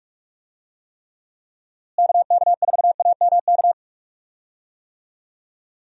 Morse code, heard as a keyed tone, about 47 wpm, 700 Hz. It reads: KK4AMK
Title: USA Call Signs - 6 Characters - Repeated 3x - 45wpm